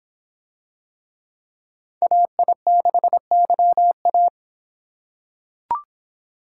Morse code, heard as a keyed tone, about 25 wpm, 700 Hz. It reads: AI6YA